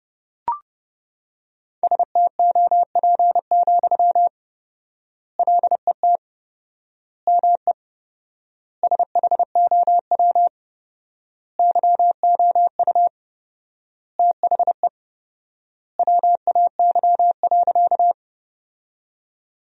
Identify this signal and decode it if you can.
Morse code, heard as a keyed tone, about 32 wpm, 700 Hz. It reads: STOP, LET ME SHOW YOU THE WAY.
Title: 4x Word-Spacing - Sentences from Top 300 Words - 30wpm